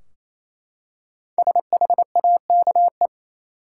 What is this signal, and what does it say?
Morse code, heard as a keyed tone, about 28 wpm, 700 Hz. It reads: SHAKE
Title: Five Letter Words - 28wpm